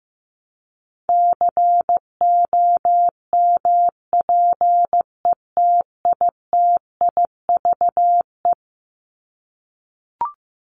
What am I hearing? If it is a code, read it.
Morse code, heard as a keyed tone, about 15 wpm, 700 Hz. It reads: COMPETITIVE
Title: Top 1001-1250 Words 15wpm